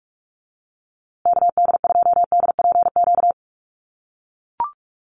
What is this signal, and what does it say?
Morse code, heard as a keyed tone, about 44 wpm, 700 Hz. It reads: KD2DPQ
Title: USA Call Signs Repeated 2x 45wpm